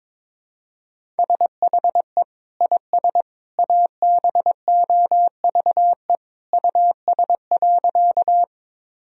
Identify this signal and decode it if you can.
Morse code, heard as a keyed tone, about 22 wpm, 700 Hz. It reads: SHE IS ABOVE US.
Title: Sentences from Top 1000 Words - Review Part 2 - 22wpm